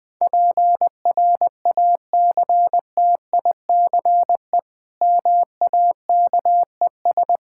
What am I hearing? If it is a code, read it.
Morse code, heard as a keyed tone, about 21 wpm, 700 Hz. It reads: PRACTICE MAKES
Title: Difficult Sentences with Hints - 20wpm